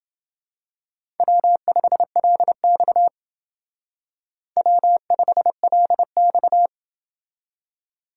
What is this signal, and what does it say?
Morse code, heard as a keyed tone, about 29 wpm, 700 Hz. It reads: W5LX W5LX